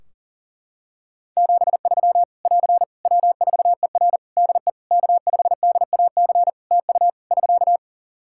Morse code, heard as a keyed tone, about 37 wpm, 700 Hz. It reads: 73 <AR> W4ER DE K5DAC TU <SK>
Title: K1USN SST QSO Practice 40wpm